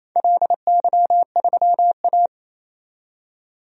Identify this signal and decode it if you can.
Morse code, heard as a keyed tone, about 28 wpm, 700 Hz. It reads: LY3A
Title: DXCC-100 Call Signs - 4 Characters - 28wpm